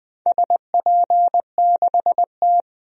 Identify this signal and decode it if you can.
Morse code, heard as a keyed tone, about 20 wpm, 700 Hz. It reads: SP6T
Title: DXCC-100 Call Signs - 4 Characters - 20wpm